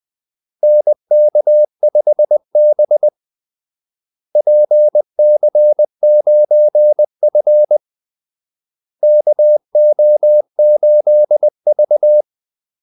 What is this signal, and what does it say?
Morse code, heard as a keyed tone, about 20 wpm, 600 Hz. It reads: NK5B PC9F KO8V
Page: Kaldesignaler 20 wpm